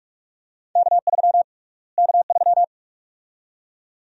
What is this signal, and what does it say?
Morse code, heard as a keyed tone, about 46 wpm, 700 Hz. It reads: K3 K3
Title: USA Call Sign Prefixes - Repeated 2x - 45wpm